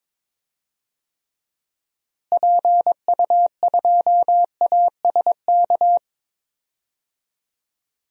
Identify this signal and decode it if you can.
Morse code, heard as a keyed tone, about 22 wpm, 700 Hz. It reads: PU2ASK
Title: DXCC-100 Call Signs - 6 Characters - Repeated 3x - 22wpm